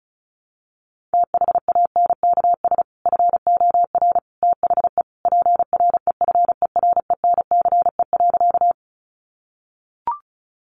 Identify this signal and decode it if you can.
Morse code, heard as a keyed tone, about 34 wpm, 700 Hz. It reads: THANKS FOR THE PREFERENCE.
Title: Sentences from Top 1500 Words 35wpm